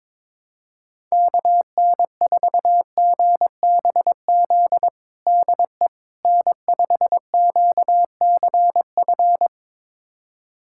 Morse code, heard as a keyed tone, about 22 wpm, 700 Hz. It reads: KN4GBZ DE N5QCF
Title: USA Callsign DE Callsign 22wpm